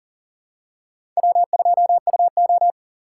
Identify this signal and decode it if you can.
Morse code, heard as a keyed tone, about 37 wpm, 700 Hz. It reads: W2UO
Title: USA Call Signs - 3x Character Spacing - 40wpm